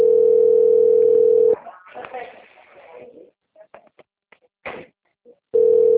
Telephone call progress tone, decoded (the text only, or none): ringback tone